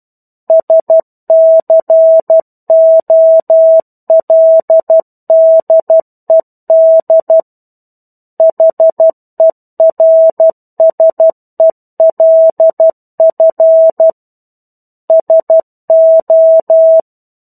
Morse code, heard as a keyed tone, about 12 wpm, 650 Hz. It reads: SCOLDED HERSELF SO
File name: 01-Chapter-01 12 WPM